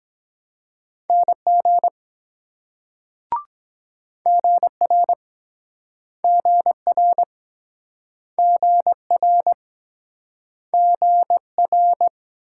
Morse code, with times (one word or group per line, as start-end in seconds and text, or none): 1.1-1.9 NG
4.3-5.1 GR
6.2-7.2 GR
8.4-9.5 GR
10.7-12.1 GR